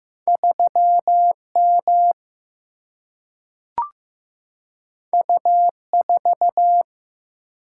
Morse code, heard as a keyed tone, about 15 wpm, 700 Hz. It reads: 3M U4
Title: Single-Letter Single-Number 15wpm